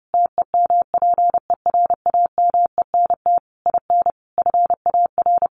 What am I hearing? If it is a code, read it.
Morse code, heard as a keyed tone, about 30 wpm, 700 Hz. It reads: TEMPERAMENT IN FAR
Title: A Sound Of Thunder 30wpm